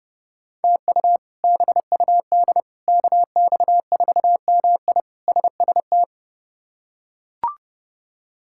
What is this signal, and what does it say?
Morse code, heard as a keyed tone, about 30 wpm, 700 Hz. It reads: TU BUD KX4MI SST
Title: K1USN SST QSO Practice - 30wpm